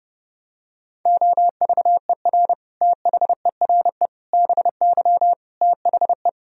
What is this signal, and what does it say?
Morse code, heard as a keyed tone, about 32 wpm, 700 Hz. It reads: OVER THERE BY THE
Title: Sentences from Top 1000 Words - Review Part 3 - 30wpm